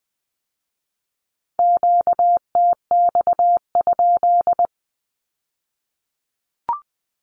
Morse code, heard as a keyed tone, about 20 wpm, 700 Hz. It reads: QTX?
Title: Q-Codes with Spoken Meaning - 20wpm